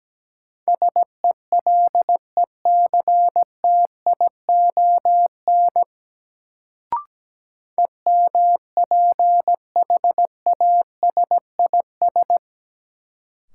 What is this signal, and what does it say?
Morse code, heard as a keyed tone, about 17 wpm, 700 Hz. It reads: SELECTION EMPHASIS
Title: Sentences from Top 1400 Words - New Word Drill - 17wpm